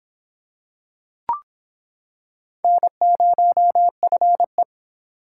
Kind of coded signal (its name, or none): Morse code